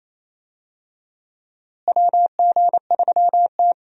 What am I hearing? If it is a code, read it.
Morse code, heard as a keyed tone, about 28 wpm, 700 Hz. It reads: WG3T
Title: USA Call Signs - 4 Characters - Repeated 2x - 28wpm